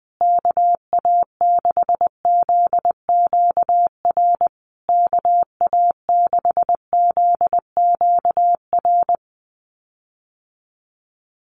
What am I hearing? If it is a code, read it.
Morse code, heard as a keyed tone, about 20 wpm, 700 Hz. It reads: KA6ZQR KA6ZQR
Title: USA Call Signs Repeated 3x 20wpm